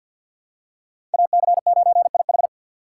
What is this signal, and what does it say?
Morse code, heard as a keyed tone, about 51 wpm, 700 Hz. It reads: AC9IH
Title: USA Call Signs - 2x Character Spacing - 50wpm